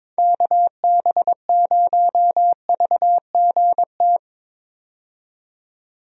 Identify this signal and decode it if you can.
Morse code, heard as a keyed tone, about 22 wpm, 700 Hz. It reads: KB0VGT
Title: USA Call Signs - 6 Characters - 22wpm